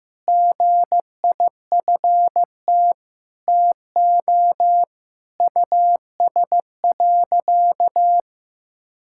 Morse code, heard as a keyed tone, about 16 wpm, 700 Hz. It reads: GIFT TO US.
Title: Sentences from Top 1000 Words - Review Part 6 - 15wpm